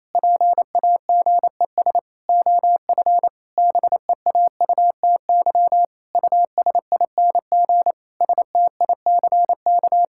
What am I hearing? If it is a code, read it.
Morse code, heard as a keyed tone, about 28 wpm, 700 Hz. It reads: PAGES OF BEAUTY USING STICK
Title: Pangram Challenge v2 - 28wpm